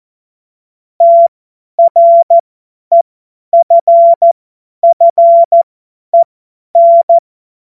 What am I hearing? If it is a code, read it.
Morse code, heard as a keyed tone, about 14 wpm, spaced as for 10 wpm, 680 Hz. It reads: TREFFEN
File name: Witz-09-September-2025-50-F70